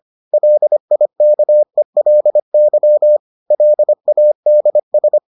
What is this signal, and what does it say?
Morse code, heard as a keyed tone, about 25 wpm, 600 Hz. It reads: LIKELY LADS